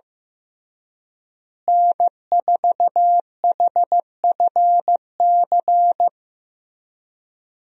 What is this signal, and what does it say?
Morse code, heard as a keyed tone, about 15 wpm, 700 Hz. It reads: N4HFC
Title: USA Call Signs - 5 Characters - Repeated 3x - 15wpm